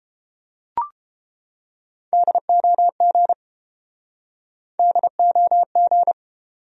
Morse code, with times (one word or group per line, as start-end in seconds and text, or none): 2.1-3.3 DOG
4.8-6.1 DOG